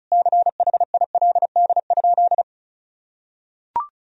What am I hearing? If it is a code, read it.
Morse code, heard as a keyed tone, about 34 wpm, 700 Hz. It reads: CHILD?
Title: Sentences from Top 1000 Words - Review Part 2 - 35wpm